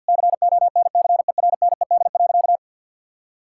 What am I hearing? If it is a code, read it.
Morse code, heard as a keyed tone, about 50 wpm, 700 Hz. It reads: CONCERNED.
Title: Sentences from Top 1000 Words - Review Part 6 - 50wpm